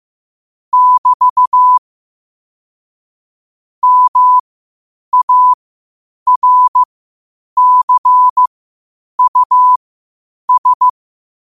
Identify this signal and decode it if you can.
Morse code, heard as a keyed tone, about 15 wpm, spaced as for 8 wpm, 1000 Hz. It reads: <BT> MARCUS
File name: qotdfiles / 8wpm